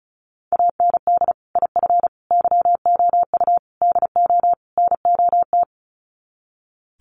Morse code, heard as a keyed tone, about 34 wpm, 700 Hz. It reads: AND IF YOU DO NOT
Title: Sets of 5 Words 35wpm